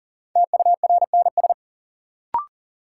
Morse code, heard as a keyed tone, about 40 wpm, 700 Hz. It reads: TURNS
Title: Sets of 5 Words Encore - New Word Drill - 40wpm